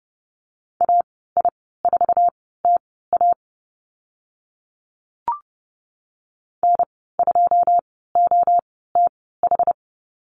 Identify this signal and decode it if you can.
Morse code, heard as a keyed tone, about 30 wpm, spaced as for 17 wpm, 700 Hz. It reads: AI4TA N2OTH
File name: Lesson 14g - Number 2 - Call Signs 30wpm